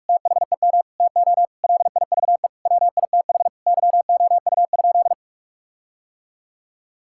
Morse code, heard as a keyed tone, about 41 wpm, 700 Hz. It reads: THEM TO LIVE WITH YOU?
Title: Sentences from Top 1600 Words 45wpm